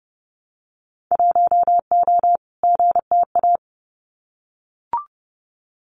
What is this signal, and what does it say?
Morse code, heard as a keyed tone, about 30 wpm, 700 Hz. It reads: 1O GTA
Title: Winter Field Day Exchanges 30wpm